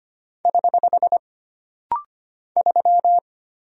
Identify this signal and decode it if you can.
Morse code, heard as a keyed tone, about 25 wpm, 700 Hz. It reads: <HH> 3
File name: Single Character (Letter:Number:Punctuation:Prosign) - Rapid-Fire - 25wpm